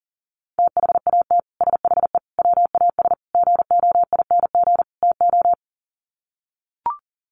Morse code, heard as a keyed tone, about 40 wpm, 700 Hz. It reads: THAT SHE WAS GOING TO
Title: Sets of 5 Words Encore2 40wpm